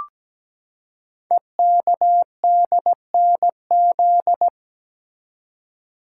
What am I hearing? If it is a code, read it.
Morse code, heard as a keyed tone, about 17 wpm, 700 Hz. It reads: EKDNZ
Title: Code Groups - 5 Letters - 17wpm